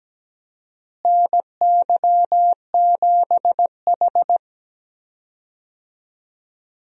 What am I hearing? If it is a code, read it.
Morse code, heard as a keyed tone, about 17 wpm, 700 Hz. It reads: NY7H
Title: USA Call Signs - 4 Characters - Repeated 2x - 17wpm